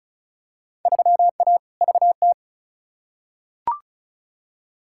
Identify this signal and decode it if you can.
Morse code, heard as a keyed tone, about 36 wpm, 700 Hz. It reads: 3A VT
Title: Field Day Exchanges Encore - 35wpm